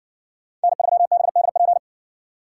Morse code, heard as a keyed tone, about 56 wpm, 700 Hz. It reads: N3BDP